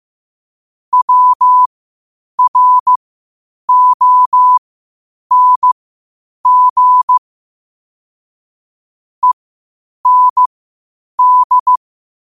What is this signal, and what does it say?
Morse code, heard as a keyed tone, about 15 wpm, spaced as for 8 wpm, 1000 Hz. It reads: WRONG END